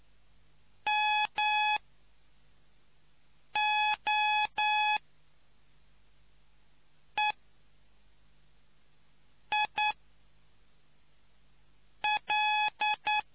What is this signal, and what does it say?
Morse code, heard as a keyed tone, about 9 wpm, spaced as for 3 wpm, 850 Hz. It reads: MOEIL